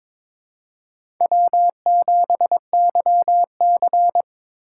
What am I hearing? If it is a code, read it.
Morse code, heard as a keyed tone, about 22 wpm, 700 Hz. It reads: W7YC
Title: USA Call Signs - 4 Characters - 3x Character Spacing - 22wpm